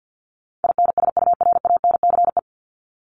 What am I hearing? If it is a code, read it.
Morse code, heard as a keyed tone, about 50 wpm, 700 Hz. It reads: INSURANCE
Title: Top 1001-1250 Words 50wpm